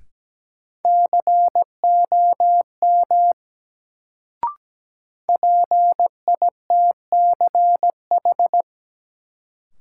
Morse code, heard as a keyed tone, about 17 wpm, 700 Hz. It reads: COM PITCH
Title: Sentences from Top 1300 Words - New Word Drill - 17wpm